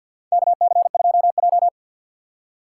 Morse code, heard as a keyed tone, about 46 wpm, 700 Hz. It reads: KC2J
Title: USA Call Signs - 2x Character Spacing - 50wpm